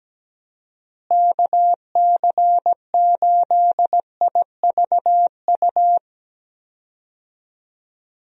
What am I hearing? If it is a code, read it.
Morse code, heard as a keyed tone, about 17 wpm, 700 Hz. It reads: KC8IVU
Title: USA Call Signs - 6 Characters - Repeated 3x - 17wpm